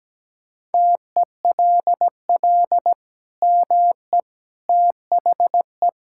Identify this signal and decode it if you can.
Morse code, heard as a keyed tone, about 17 wpm, 700 Hz. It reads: TELL ME THE
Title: Sentences from Top 1000 Words - Review Part 5 - 17wpm